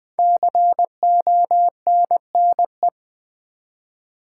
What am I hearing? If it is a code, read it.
Morse code, heard as a keyed tone, about 20 wpm, 700 Hz. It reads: CONNE
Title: Build A Word - 5 Letters At A Time - 10-Letter Words - 20wpm